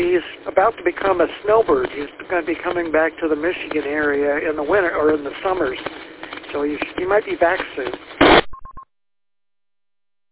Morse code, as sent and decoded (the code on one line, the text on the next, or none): none